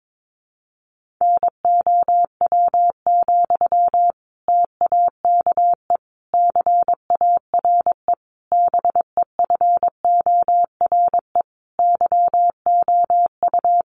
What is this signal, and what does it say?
Morse code, heard as a keyed tone, about 22 wpm, 700 Hz. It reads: NOW, TAKE CARE BEFORE YOU
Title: Sentences from Top 1500 Words 22wpm